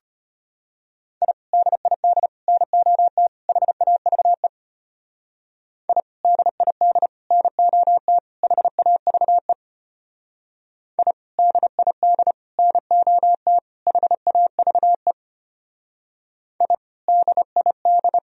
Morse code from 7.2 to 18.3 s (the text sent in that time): NOT HAVE I DID NOT HAVE I DID